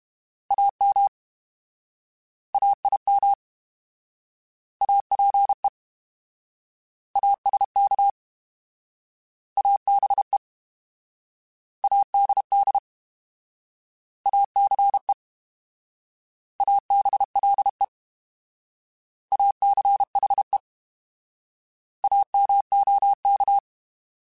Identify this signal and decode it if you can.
Morse code, heard as a keyed tone, about 31 wpm, 800 Hz. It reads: AM AIM APE ASK ABE ADD ACE ABLE ACHE AMOK
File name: Alphabetical / Alphabetical mp3s / As @30